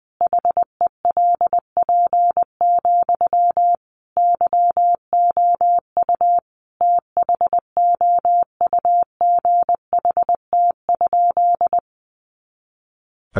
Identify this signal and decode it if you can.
Morse code, heard as a keyed tone, about 20 wpm, 700 Hz. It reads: HELP, YOU THOUGHT?
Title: Sentences from Top 300 Words 20wpm